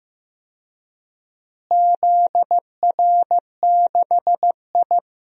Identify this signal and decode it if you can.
Morse code, heard as a keyed tone, about 15 wpm, 700 Hz. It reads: ZR6I
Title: DXCC-100 Call Signs - 4 Characters - Repeated 3x - 15wpm